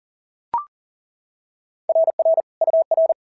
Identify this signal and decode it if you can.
Morse code, heard as a keyed tone, about 42 wpm, 650 Hz. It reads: RR UR
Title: SOTA QSO Practice 40wpm